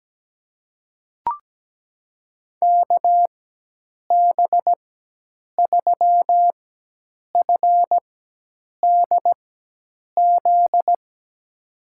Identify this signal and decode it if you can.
Morse code, heard as a keyed tone, about 17 wpm, spaced as for 8 wpm, 700 Hz. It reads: KB3FDZ
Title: USA Call Signs - 6 Characters - 4x Character Spacing - 17wpm